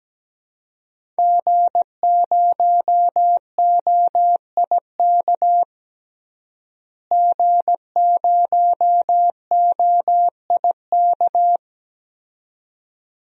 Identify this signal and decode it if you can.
Morse code, heard as a keyed tone, about 17 wpm, 700 Hz. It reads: G0OIK G0OIK
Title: DXCC-100 Call Signs - 5 Characters - Repeated 2x - 17wpm